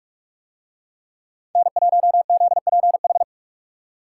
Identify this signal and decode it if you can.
Morse code, heard as a keyed tone, about 46 wpm, 700 Hz. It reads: N1ZPH